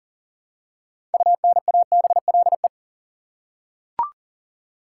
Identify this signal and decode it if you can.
Morse code, heard as a keyed tone, about 37 wpm, 700 Hz. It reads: UNABLE
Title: Six Letter Words - 40wpm